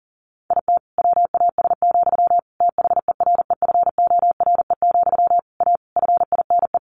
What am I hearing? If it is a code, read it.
Morse code, heard as a keyed tone, about 40 wpm, 700 Hz. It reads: IT WAS, THEREFORE, A FINE
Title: Sentences from Top 1400 Words 40wpm